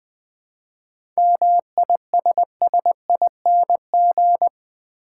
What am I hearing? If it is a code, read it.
Morse code, heard as a keyed tone, about 20 wpm, 700 Hz. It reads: MISSING
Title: 7 Letter Words - First 3 Letters Given - 20wpm